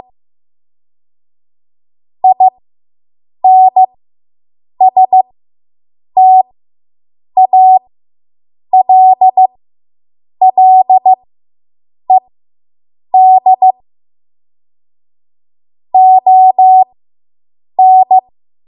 Morse code, heard as a keyed tone, about 15 wpm, spaced as for 7 wpm, 750 Hz. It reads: INSTALLED ON